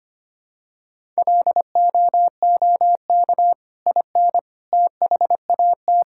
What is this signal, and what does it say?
Morse code, heard as a keyed tone, about 25 wpm, 700 Hz. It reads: LOOK IN THAT
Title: Sentences from Top 1000 Words - Review Part 5 - 25wpm